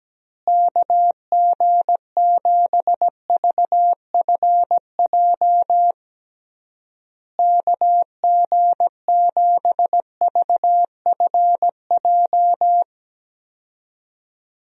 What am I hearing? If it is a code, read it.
Morse code, heard as a keyed tone, about 17 wpm, 700 Hz. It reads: KG7VFJ KG7VFJ
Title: USA Call Signs - 6 Characters - Repeated 3x - 17wpm